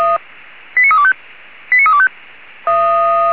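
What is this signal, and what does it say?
Five-tone selective call (ZVEI), heard as quick runs of five tones, tones of about 70 ms. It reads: 89325 89325
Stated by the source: ZVEI five-tone call received on SDR, channel 464.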